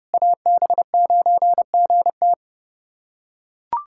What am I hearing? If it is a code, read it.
Morse code, heard as a keyed tone, about 29 wpm, 700 Hz. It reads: AB9GT
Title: USA Callsign DE Callsign - 5 Characters - 30wpm